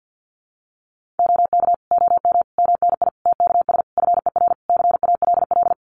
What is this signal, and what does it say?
Morse code, heard as a keyed tone, about 50 wpm, 700 Hz. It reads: OK OM MNI TKS FER CALL
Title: Common QSO Phrases 50wpm